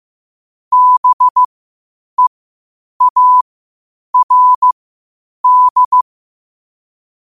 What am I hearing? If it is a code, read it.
Morse code, heard as a keyed tone, about 15 wpm, spaced as for 8 wpm, 1000 Hz. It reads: BEARD
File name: qotdfiles / 8wpm